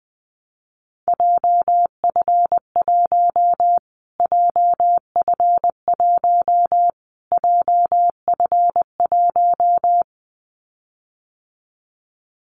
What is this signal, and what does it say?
Morse code, heard as a keyed tone, about 20 wpm, 700 Hz. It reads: JF1 JF1 JF1
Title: Callsign Prefixes for Top 25 DXCC - Repeated 3x - 20wpm